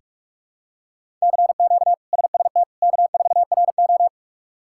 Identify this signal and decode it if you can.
Morse code, heard as a keyed tone, about 43 wpm, 700 Hz. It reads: CQ SST K4RO
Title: K1USN SST QSO Practice - 45wpm